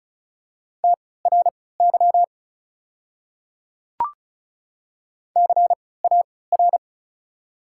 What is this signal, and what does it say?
Morse code, heard as a keyed tone, about 36 wpm, spaced as for 20 wpm, 700 Hz. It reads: TRY CAR
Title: 3x Character Spacing - Top 500 Words - 35wpm